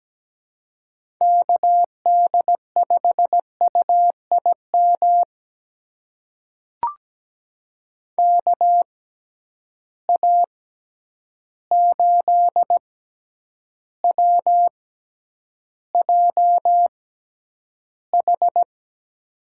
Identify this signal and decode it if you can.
Morse code, heard as a keyed tone, about 17 wpm, spaced as for 9 wpm, 700 Hz. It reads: KD5UIM KA8WJH
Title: USA Call Signs - 6 Characters - 6x Character Spacing - 17wpm